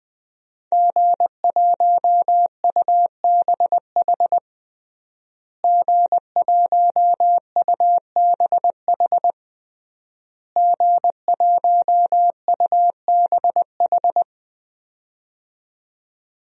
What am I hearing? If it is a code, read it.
Morse code, heard as a keyed tone, about 21 wpm, 700 Hz. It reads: G1UBH G1UBH G1UBH
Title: DXCC-100 Call Signs - 5 Characters - Repeated 3x - 20wpm